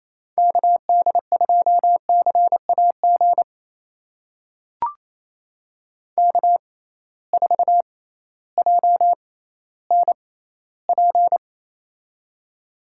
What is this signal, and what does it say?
Morse code, heard as a keyed tone, about 28 wpm, spaced as for 15 wpm, 700 Hz. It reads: KD2CAG K4JNP
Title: USA Call Signs - 6x Character Spacing - 28wpm